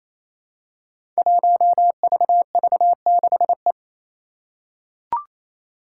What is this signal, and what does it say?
Morse code, heard as a keyed tone, about 27 wpm, 700 Hz. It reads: 1VV6E
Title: Five Challenging Letter-Number Combinations 28wpm